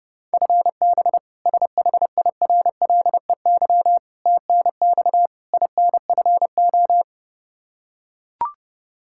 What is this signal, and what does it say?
Morse code, heard as a keyed tone, about 31 wpm, 700 Hz. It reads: FB SHIRLEY TNX INFO
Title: K1USN SST QSO Practice 30wpm